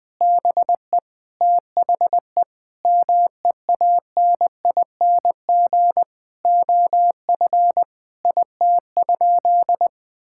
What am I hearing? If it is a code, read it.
Morse code, heard as a keyed tone, about 20 wpm, 700 Hz. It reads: BE THE MEANING OF IT?
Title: Sentences from Top 1000 Words - Review Part 6 - 20wpm